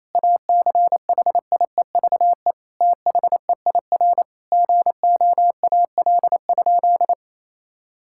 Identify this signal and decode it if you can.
Morse code, heard as a keyed tone, about 27 wpm, 700 Hz. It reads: ACHIEVE THEIR GOAL?
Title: Sentences from Top 1000 Words - Review Part 6 - 28wpm